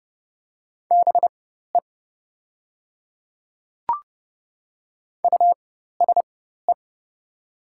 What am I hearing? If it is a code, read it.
Morse code, heard as a keyed tone, about 30 wpm, spaced as for 14 wpm, 700 Hz. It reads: BE USE
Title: 4x Character Spacing - Top 300 Words - 30wpm